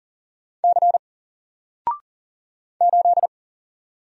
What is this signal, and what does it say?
Morse code, heard as a keyed tone, about 40 wpm, 700 Hz. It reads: C 8
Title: Single Character (Letter:Number:Punctuation:Prosign) 40wpm